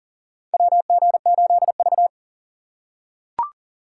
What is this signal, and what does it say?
Morse code, heard as a keyed tone, about 40 wpm, 700 Hz. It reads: WG8V
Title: USA Call Signs - 4 Characters - 4x Character Spacing - 40wpm